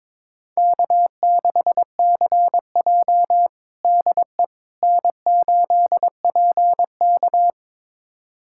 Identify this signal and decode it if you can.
Morse code, heard as a keyed tone, about 22 wpm, 700 Hz. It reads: K6CJ DE N8PK
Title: USA Callsign DE Callsign - 4 Characters - 22wpm